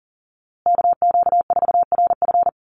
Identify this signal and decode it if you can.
Morse code, heard as a keyed tone, about 40 wpm, 700 Hz. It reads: KQ4RF
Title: USA Call Signs Repeated 3x 40wpm